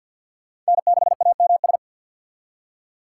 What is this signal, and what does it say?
Morse code, heard as a keyed tone, about 61 wpm, 700 Hz. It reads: N6AMS
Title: Speed Racing - USA 1x3 Call Signs 50wpm